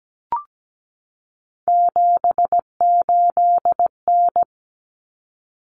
Morse code, heard as a keyed tone, about 17 wpm, 690 Hz. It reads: 78N
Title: Common Cut Numbers - 3-Digit 17wpm